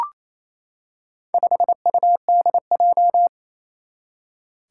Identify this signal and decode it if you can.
Morse code, heard as a keyed tone, about 28 wpm, 700 Hz. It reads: 5UDJ
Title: Four Challenging Letter-Number Combinations 28wpm